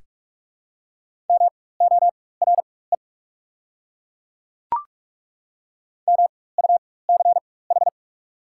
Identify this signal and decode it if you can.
Morse code, heard as a keyed tone, about 43 wpm, spaced as for 21 wpm, 690 Hz. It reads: MORE MUCH
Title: 4x Character Spacing - Top 300 Words - 45wpm